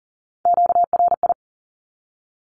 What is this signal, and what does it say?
Morse code, heard as a keyed tone, about 40 wpm, 700 Hz. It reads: QRI
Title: Q-Codes 40wpm